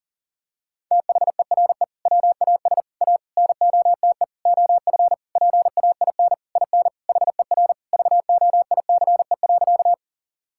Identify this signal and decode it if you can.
Morse code, heard as a keyed tone, about 40 wpm, 700 Hz. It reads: THERE WAS A NOTE OF PAIN IN HER VOICE.
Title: Sentences from Top 1600 Words 40wpm